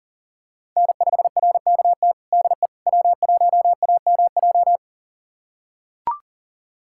Morse code, heard as a keyed tone, about 40 wpm, 700 Hz. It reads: N5RKT DE W1AMJ
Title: USA Callsign DE Callsign - 5 Characters - 40wpm